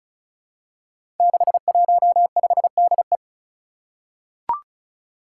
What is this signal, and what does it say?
Morse code, heard as a keyed tone, about 34 wpm, 700 Hz. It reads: 615DE
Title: Five Challenging Letter-Number Combinations 35wpm